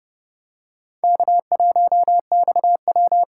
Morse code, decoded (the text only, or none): K1XW